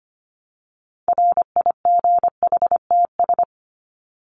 Morse code, heard as a keyed tone, about 25 wpm, 700 Hz. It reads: RIGHTS
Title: ES:S Suffix - Top 100 Words with Base Words 25wpm